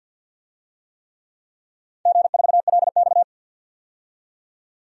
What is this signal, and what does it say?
Morse code, heard as a keyed tone, about 48 wpm, 700 Hz. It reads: G4LX
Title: DXCC-100 Call Signs - 4 Characters - 50wpm